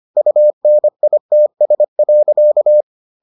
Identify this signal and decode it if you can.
Morse code, heard as a keyed tone, about 25 wpm, 600 Hz. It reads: UNITS.